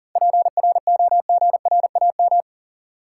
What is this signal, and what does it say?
Morse code, heard as a keyed tone, about 40 wpm, 700 Hz. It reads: PROGRAM
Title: Sentences from Top 1500 Words - New Word Drill Alt - 40wpm